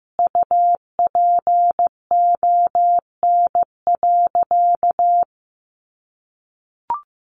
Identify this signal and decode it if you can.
Morse code, heard as a keyed tone, about 15 wpm, 700 Hz. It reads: UPON.
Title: Sentences from Top 1400 Words 15wpm